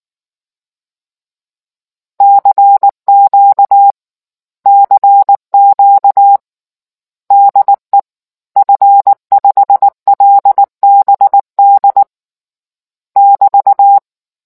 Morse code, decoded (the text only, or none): CQ CQ DE F5LBD <BT>